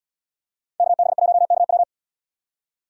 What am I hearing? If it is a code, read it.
Morse code, heard as a keyed tone, about 78 wpm, 700 Hz. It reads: KB1KW